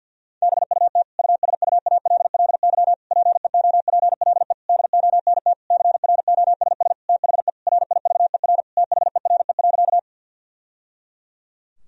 Morse code, heard as a keyed tone, about 50 wpm, 700 Hz. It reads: BUT USUALLY PEOPLE DONT CROSS THE RIVER THERE.
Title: Sentences from Top 1000 Words - Review Part 3 - 50wpm